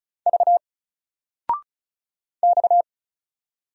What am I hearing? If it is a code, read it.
Morse code, heard as a keyed tone, about 35 wpm, 700 Hz. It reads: V X
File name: Single-Letter-Number-35wpm